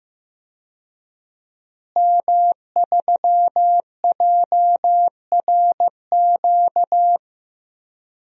Morse code, heard as a keyed tone, about 15 wpm, 700 Hz. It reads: M3JRQ
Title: DXCC-100 Call Signs - 5 Characters - Repeated 2x - 15wpm